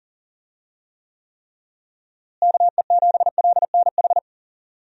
Morse code, heard as a keyed tone, about 40 wpm, 700 Hz. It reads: KE7LNH
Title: USA Call Signs - 6 Characters - Repeated 3x - 40wpm